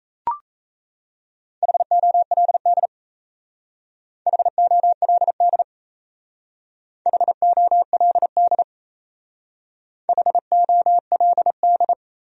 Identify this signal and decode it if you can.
Morse code, heard as a keyed tone, about 35 wpm, 700 Hz. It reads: HOLD HOLD HOLD HOLD
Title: Speed-Racing Top 500 Words 28wpm